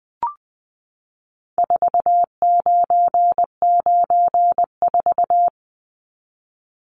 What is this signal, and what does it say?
Morse code, heard as a keyed tone, about 20 wpm, 700 Hz. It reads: 4994
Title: Numbers - 4-Digit 20wpm